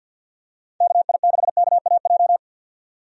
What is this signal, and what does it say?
Morse code, heard as a keyed tone, about 50 wpm, 700 Hz. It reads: KI6KAJ
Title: USA Call Signs - 6 Characters - 50wpm